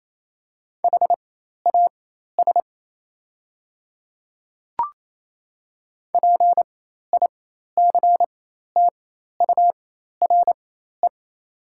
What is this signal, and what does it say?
Morse code, heard as a keyed tone, about 28 wpm, spaced as for 13 wpm, 700 Hz. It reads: HAS PICTURE
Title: 4x Character Spacing - Top 200 Words - 28wpm